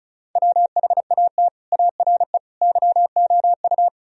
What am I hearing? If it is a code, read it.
Morse code, heard as a keyed tone, about 46 wpm, 700 Hz. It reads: WHAT ARE YOU